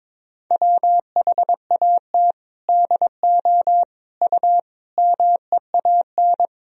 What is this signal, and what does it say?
Morse code, heard as a keyed tone, about 22 wpm, 700 Hz. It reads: WHAT DO U MEAN
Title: Sentences from Top 400 Words and Abbreviations 22wpm